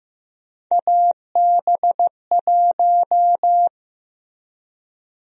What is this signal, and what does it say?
Morse code, heard as a keyed tone, about 15 wpm, 700 Hz. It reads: AB1
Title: USA Call Sign Prefixes - Repeated 2x - 15wpm